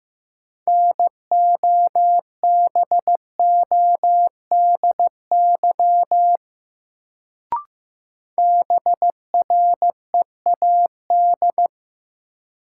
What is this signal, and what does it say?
Morse code, heard as a keyed tone, about 15 wpm, 700 Hz. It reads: NOBODY BREAD
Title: Sentences from Top 1600 Words - New Word Drill Alt - 15wpm